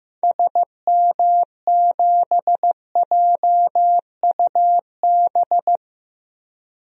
Morse code, heard as a keyed tone, about 15 wpm, 700 Hz. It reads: SM7JUB
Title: DXCC-100 Call Signs - 6 Characters - Repeated 2x - 15wpm